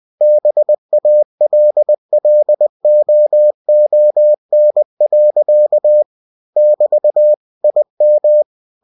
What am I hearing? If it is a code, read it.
Morse code, heard as a keyed tone, about 20 wpm, 600 Hz. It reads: BALLOON. <BT> IM